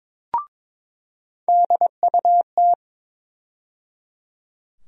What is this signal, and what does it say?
Morse code, heard as a keyed tone, about 22 wpm, 700 Hz. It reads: DUT
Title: USA Call Sign Postfixes - 3 Letters 22wpm